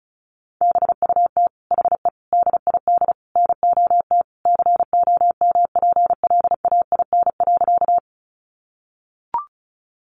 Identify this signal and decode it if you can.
Morse code, heard as a keyed tone, about 34 wpm, 700 Hz. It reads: BUT HE DID NOT COMPLAIN.
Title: Sentences from Top 1300 Words 35wpm